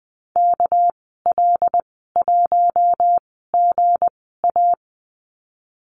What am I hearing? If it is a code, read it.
Morse code, heard as a keyed tone, about 20 wpm, spaced as for 15 wpm, 700 Hz. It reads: KL1GA
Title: 200 Lesson 32g - Letter k - Call Signs 20wpm